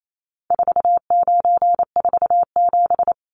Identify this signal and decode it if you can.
Morse code, heard as a keyed tone, about 28 wpm, 700 Hz. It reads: 4947
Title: Numbers - 4-Digit 28wpm